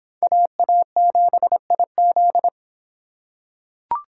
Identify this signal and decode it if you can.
Morse code, heard as a keyed tone, about 26 wpm, 700 Hz. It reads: AA7IZ